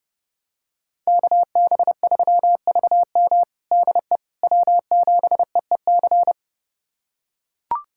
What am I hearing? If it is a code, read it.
Morse code, heard as a keyed tone, about 30 wpm, 700 Hz. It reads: KB3VM DE W7EEC
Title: USA Callsign DE Callsign - 5 Characters - 30wpm